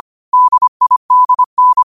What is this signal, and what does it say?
Morse code, heard as a keyed tone, about 25 wpm, 1000 Hz. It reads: DIDN
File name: qotdfiles / 25wpm